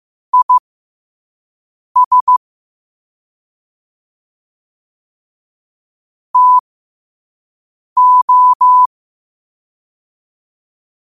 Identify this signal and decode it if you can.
Morse code, heard as a keyed tone, about 15 wpm, spaced as for 5 wpm, 1000 Hz. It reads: IS TO
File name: qotdfiles / 5wpm